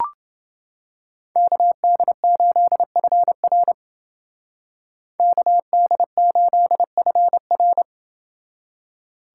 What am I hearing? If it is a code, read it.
Morse code, heard as a keyed tone, about 29 wpm, 700 Hz. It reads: KD8FR KD8FR